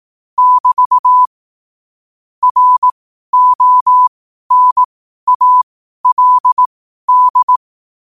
Morse code, heard as a keyed tone, about 18 wpm, spaced as for 12 wpm, 1000 Hz. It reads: <BT> RONALD
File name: qotdfiles / 12wpm